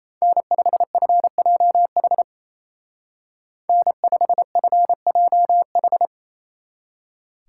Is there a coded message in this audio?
Yes: Morse code from 0.2 to 6.1 s, heard as a keyed tone, about 36 wpm, 700 Hz. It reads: N5FJH N5FJH